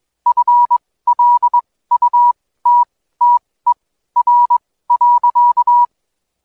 Morse code, heard as a keyed tone, about 21 wpm, spaced as for 15 wpm, 970 Hz. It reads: FLUTTER.
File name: Code Quick / AudioDownld2 / Seg125-DayDrill30-16WPM